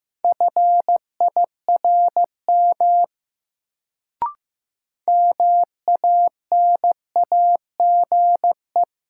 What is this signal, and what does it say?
Morse code, heard as a keyed tone, about 15 wpm, 700 Hz. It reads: FIRM MANAGE
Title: Sentences from Top 1400 Words - New Word Drill - 15wpm